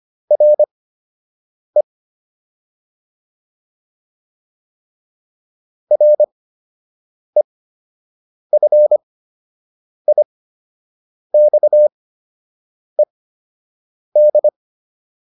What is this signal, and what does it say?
Morse code, heard as a keyed tone, about 25 wpm, spaced as for 7 wpm, 600 Hz. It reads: RE REFIXED